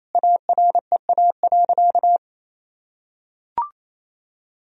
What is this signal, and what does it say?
Morse code, heard as a keyed tone, about 27 wpm, 700 Hz. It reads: AREA.
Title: Sentences from Top 1000 Words - Review Part 2 - 28wpm